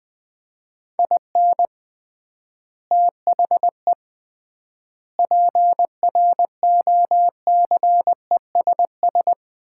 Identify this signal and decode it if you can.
Morse code, heard as a keyed tone, about 20 wpm, 700 Hz. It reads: IN THE PROCESS
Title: 3x Word-Spacing - Sets of 4 Words - 20wpm